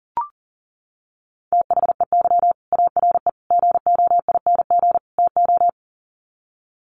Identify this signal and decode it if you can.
Morse code, heard as a keyed tone, about 40 wpm, 700 Hz. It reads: THEY ARE GOING TO
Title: Sets of 4 Words 40wpm